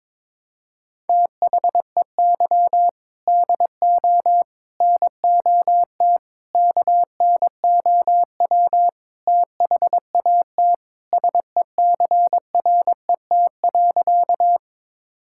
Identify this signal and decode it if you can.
Morse code, heard as a keyed tone, about 22 wpm, 700 Hz. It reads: THEY DO NOT KNOW THAT SECRET.
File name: practice / 22wpm / Sentences with 4-Word Phrases - 22wpm